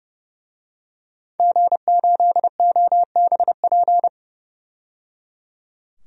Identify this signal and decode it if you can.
Morse code, heard as a keyed tone, about 30 wpm, 700 Hz. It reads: G8OBP
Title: DXCC-100 Call Signs - 5 Characters - 30wpm